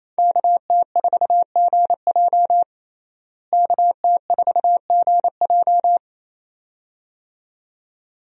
Morse code, heard as a keyed tone, about 27 wpm, 700 Hz. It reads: KT4GJ KT4GJ
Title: USA Call Signs - 5 Characters - Repeated 2x - 28wpm